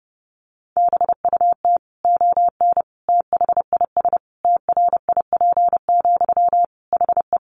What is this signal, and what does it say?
Morse code, heard as a keyed tone, about 30 wpm, 700 Hz. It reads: BUT ON THIS TRIP, HE
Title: Sentences from Top 1400 Words 30wpm